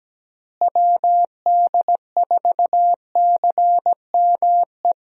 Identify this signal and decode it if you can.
Morse code, heard as a keyed tone, about 17 wpm, 700 Hz. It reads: WD4CME
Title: USA Call Signs - 6 Characters - Repeated 3x - 17wpm